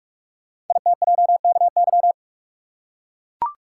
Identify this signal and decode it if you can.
Morse code, heard as a keyed tone, about 44 wpm, 700 Hz. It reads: ITJKY
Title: Code Groups - 5 Letters - 45wpm